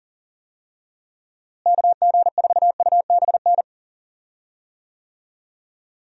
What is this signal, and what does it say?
Morse code, heard as a keyed tone, about 37 wpm, 700 Hz. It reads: KG4UBN
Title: USA Call Signs - 6 Characters - Repeated 3x - 40wpm